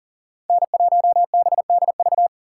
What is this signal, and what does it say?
Morse code, heard as a keyed tone, about 49 wpm, 700 Hz. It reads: N1BDV
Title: Speed Racing - USA 1x3 Call Signs 40wpm